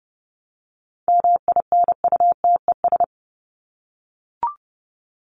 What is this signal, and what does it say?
Morse code, heard as a keyed tone, about 30 wpm, 700 Hz. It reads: MINUTES
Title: ES:S Suffix - Top 100 Words with Base Words 30wpm